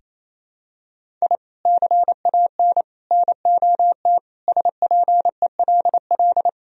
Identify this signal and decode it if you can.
Morse code, heard as a keyed tone, about 28 wpm, 700 Hz. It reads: I CAN NOT SPELL
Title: Sentences from Top 1000 Words - Review Part 1 - 28wpm